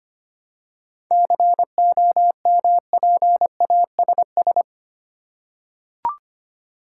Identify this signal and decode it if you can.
Morse code, heard as a keyed tone, about 25 wpm, 700 Hz. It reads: COMPASS
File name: practice / 25wpm / Easy Words with Hints - 25wpm